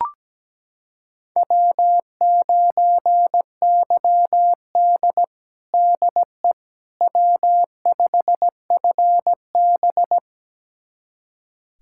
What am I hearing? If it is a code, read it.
Morse code, heard as a keyed tone, about 17 wpm, 700 Hz. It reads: W9YD DE W5FB
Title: USA Callsign DE Callsign - 4 Characters - 17wpm